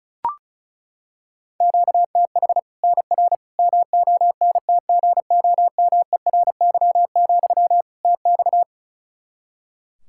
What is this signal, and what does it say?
Morse code, heard as a keyed tone, about 36 wpm, 700 Hz. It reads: QTH NR MONTGOMERY, TX
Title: Practice QSO - Exchange 1 - Beginning - 35wpm